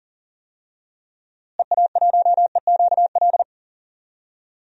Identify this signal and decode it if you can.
Morse code, heard as a keyed tone, about 40 wpm, 700 Hz. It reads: EA1EQL
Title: DXCC-100 Call Signs - 6 Characters - Repeated 2x - 40wpm